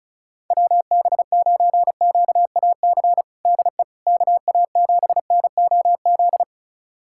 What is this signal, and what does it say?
Morse code, heard as a keyed tone, about 36 wpm, 700 Hz. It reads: WB9QAC DE KA7NOZ
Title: USA Callsign DE Callsign - 6 Characters - 35wpm